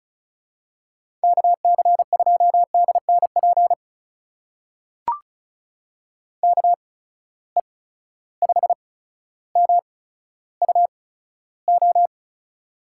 Morse code, heard as a keyed tone, about 36 wpm, spaced as for 15 wpm, 700 Hz. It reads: KC2DNP KE5MUO
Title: USA Call Signs - 6 Characters - 8x Character Spacing - 35wpm